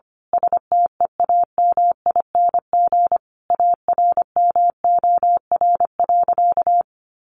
Morse code, heard as a keyed tone, about 25 wpm, 700 Hz. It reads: STEAMING ARMOR.
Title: A Sound Of Thunder 25wpm